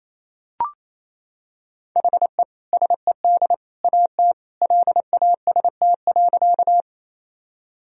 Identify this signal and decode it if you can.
Morse code, heard as a keyed tone, about 27 wpm, 700 Hz. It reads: HE SED AT LAST.
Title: Sentences from Top 300 Words and Abbreviations 28wpm